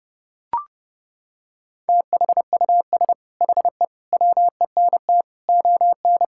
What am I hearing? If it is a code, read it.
Morse code, heard as a keyed tone, about 30 wpm, 700 Hz. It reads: THUS HE WENT ON
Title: Sentences from Top 1000 Words - Review Part 5 - 30wpm